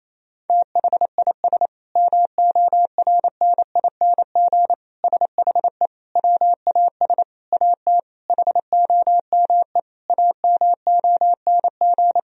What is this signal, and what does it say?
Morse code, heard as a keyed tone, about 28 wpm, 700 Hz. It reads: THIS MORNING SHE WAS AT HOME AMONG
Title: Sentences from Top 1000 Words - Review Part 3 - 28wpm